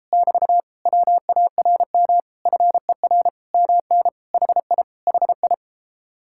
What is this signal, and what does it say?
Morse code, heard as a keyed tone, about 33 wpm, 700 Hz. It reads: <BT> WARM FER MN HI HI